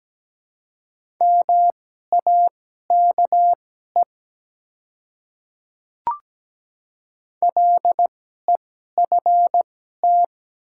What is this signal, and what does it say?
Morse code, heard as a keyed tone, about 17 wpm, spaced as for 12 wpm, 700 Hz. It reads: MAKE LEFT
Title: 2x Character Spacing - Top 300 Words - 17wpm